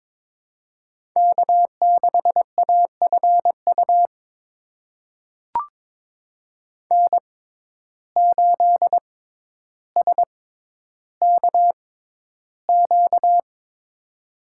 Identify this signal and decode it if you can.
Morse code, heard as a keyed tone, about 22 wpm, spaced as for 11 wpm, 700 Hz. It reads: K6AFU N8SKQ
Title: USA Call Signs - 5 Characters - 6x Character Spacing - 22wpm